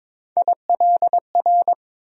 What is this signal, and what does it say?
Morse code, heard as a keyed tone, about 21 wpm, 700 Hz. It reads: ILR
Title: USA Call Sign Postfixes - 3 Letters 22wpm